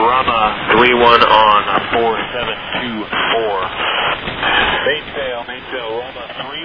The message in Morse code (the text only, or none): none